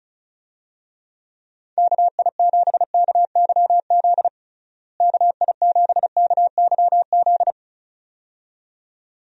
Morse code, heard as a keyed tone, about 34 wpm, 700 Hz. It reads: KI7KYZ KI7KYZ
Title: USA Call Signs - 6 Characters - Repeated 2x - 35wpm